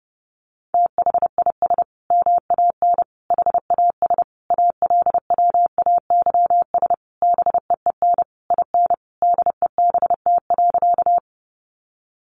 Morse code, heard as a keyed tone, about 30 wpm, 700 Hz. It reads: THIS MAN HAS ALWAYS BEEN IN DEBT.
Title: Sentences from Top 1300 Words 30wpm